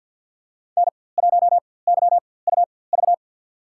Morse code, heard as a keyed tone, about 53 wpm, spaced as for 24 wpm, 700 Hz. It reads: N1YUV
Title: USA Call Signs - 5 Characters - 4x Character Spacing - 50wpm